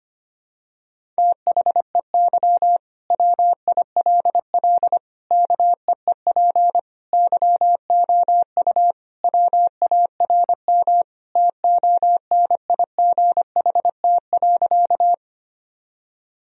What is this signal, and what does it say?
Morse code, heard as a keyed tone, about 25 wpm, 700 Hz. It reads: THEY WILL KEEP YOU WARM TONIGHT.
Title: Sentences from Top 1000 Words - Review Part 3 - 25wpm